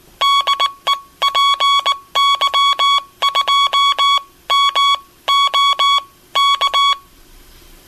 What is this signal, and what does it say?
Morse code, heard as a keyed tone, about 21 wpm, spaced as for 17 wpm, 1120 Hz. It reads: DE PY2MO K
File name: Aniversario 25 anos Encontro Dx- Radio aparecida